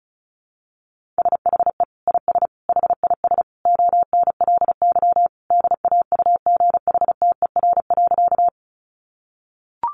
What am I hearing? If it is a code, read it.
Morse code, heard as a keyed tone, about 34 wpm, 700 Hz. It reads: SHE IS HIS ONLY DAUGHTER.
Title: Sentences from Top 1500 Words 35wpm